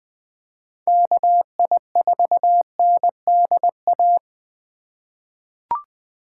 Morse code, heard as a keyed tone, about 21 wpm, 700 Hz. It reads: KI4NDA
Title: USA Call Signs - 6 Characters - 2x Character Spacing - 20wpm